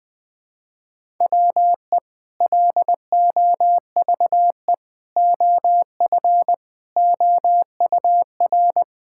Morse code, heard as a keyed tone, about 20 wpm, 700 Hz. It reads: WE LOVE OF OUR
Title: Sentences with 2-Word Phrases - 20wpm